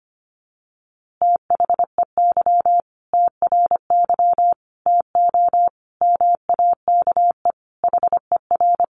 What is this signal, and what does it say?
Morse code, heard as a keyed tone, about 25 wpm, 700 Hz. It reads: THEY TRY TO MAKE HER
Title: Easy Phrases 25wpm